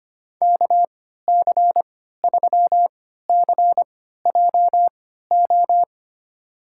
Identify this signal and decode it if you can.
Morse code, heard as a keyed tone, about 25 wpm, spaced as for 14 wpm, 700 Hz. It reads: KC3CJO
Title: USA Call Signs - 6 Characters - 3x Character Spacing - 25wpm